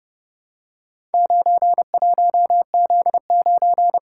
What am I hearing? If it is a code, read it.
Morse code, heard as a keyed tone, about 30 wpm, 700 Hz. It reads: 91Z9